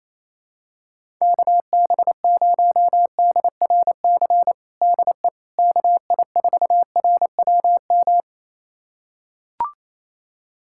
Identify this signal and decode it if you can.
Morse code, heard as a keyed tone, about 28 wpm, 700 Hz. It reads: KB0DRC DE KI4RWM
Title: USA Callsign DE Callsign - 6 Characters - 28wpm